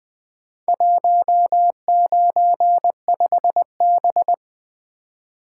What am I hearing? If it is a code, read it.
Morse code, heard as a keyed tone, about 20 wpm, 700 Hz. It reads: 195B
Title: Four Letter-Number Combinations - 20wpm